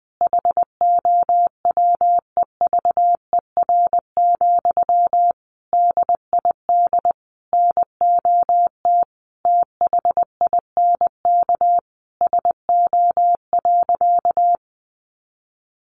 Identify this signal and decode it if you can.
Morse code, heard as a keyed tone, about 20 wpm, 700 Hz. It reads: HOWEVER, DID NOT THINK SO.
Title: Sentences from Top 1500 Words 20wpm